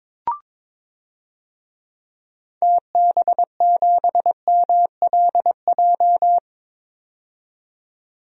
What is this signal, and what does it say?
Morse code, heard as a keyed tone, about 22 wpm, 700 Hz. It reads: TB7MLJ
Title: DXCC-100 Call Signs - 6 Characters - 22wpm